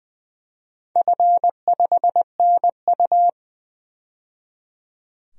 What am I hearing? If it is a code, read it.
Morse code, heard as a keyed tone, about 20 wpm, 700 Hz. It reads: F5NU
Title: DXCC-100 Call Signs - 4 Characters - 20wpm